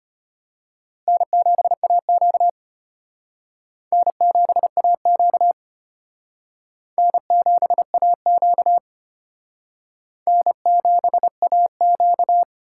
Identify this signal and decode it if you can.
Morse code, heard as a keyed tone, about 32 wpm, 700 Hz. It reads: N7AQ N7AQ N7AQ N7AQ